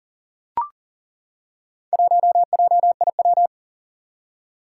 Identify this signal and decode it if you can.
Morse code, heard as a keyed tone, about 41 wpm, 700 Hz. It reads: 1JIW
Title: Four Challenging Letter-Number Combinations 40wpm